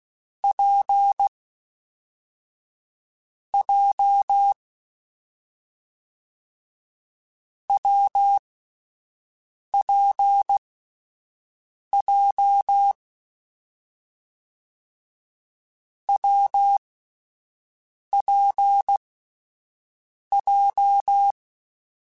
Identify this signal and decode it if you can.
Morse code, heard as a keyed tone, about 16 wpm, 800 Hz. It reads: P J W P J W P J
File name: Difficult Combos / Difficult Combos-mp3s / WPJ WPJ1 @10